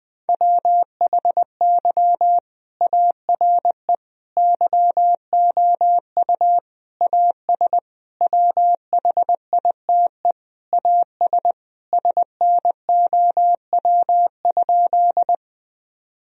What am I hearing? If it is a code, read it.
Morse code, heard as a keyed tone, about 21 wpm, 700 Hz. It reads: WHY ARE YOU AS WHITE AS SNOW?
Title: Sentences from Top 1000 Words - Review Part 3 - 20wpm